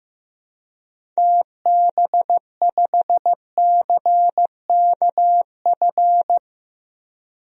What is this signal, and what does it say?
Morse code, heard as a keyed tone, about 15 wpm, 700 Hz. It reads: TB5CKF
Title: DXCC-100 Call Signs - 6 Characters - Repeated 2x - 15wpm